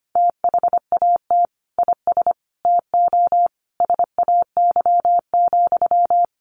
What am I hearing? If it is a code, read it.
Morse code, heard as a keyed tone, about 25 wpm, 700 Hz. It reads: THAT IS TO SAY,
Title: Sentences from Top 700 Words 25wpm